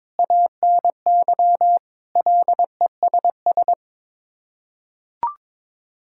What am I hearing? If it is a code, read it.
Morse code, heard as a keyed tone, about 21 wpm, 700 Hz. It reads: ANY LESS
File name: practice / 22wpm / Sets of 7 Words - 22wpm